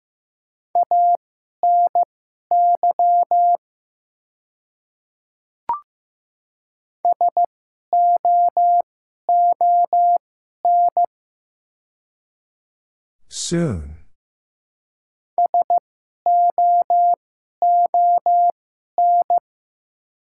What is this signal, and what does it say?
Morse code, heard as a keyed tone, about 15 wpm, spaced as for 11 wpm, 700 Hz. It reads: ANY SOON SOON
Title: 2x Character Spacing - Top 500 Words - 15wpm